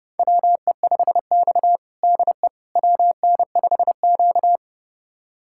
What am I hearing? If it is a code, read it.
Morse code, heard as a keyed tone, about 31 wpm, 700 Hz. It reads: WE5X DE WN5Q
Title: USA Callsign DE Callsign - 4 Characters - 30wpm